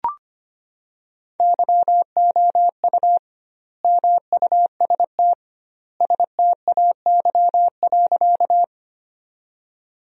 Morse code, heard as a keyed tone, about 25 wpm, 700 Hz. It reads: YOU MUST STAY.
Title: 2x Word-Spacing - Sentences from Top 500 Words - 25wpm